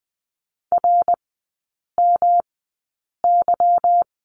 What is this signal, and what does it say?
Morse code, heard as a keyed tone, about 20 wpm, spaced as for 8 wpm, 690 Hz. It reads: RMY
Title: 145 Lesson 24e - Letter y - Three Characters 20wpm